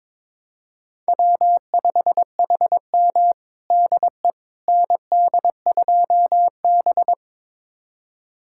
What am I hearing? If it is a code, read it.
Morse code, heard as a keyed tone, about 22 wpm, 700 Hz. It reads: W5HM DE ND2B
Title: USA Callsign DE Callsign - 4 Characters - 22wpm